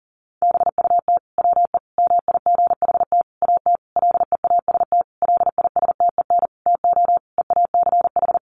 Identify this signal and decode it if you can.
Morse code, heard as a keyed tone, about 37 wpm, 700 Hz. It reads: BUT WE MIGHT AT LEAST LISTEN TO EACH
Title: Sentences from Top 1300 Words 40wpm